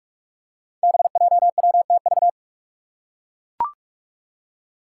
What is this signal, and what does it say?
Morse code, heard as a keyed tone, about 41 wpm, 700 Hz. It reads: BJWTV
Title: Five Challenging Letter-Number Combinations 45wpm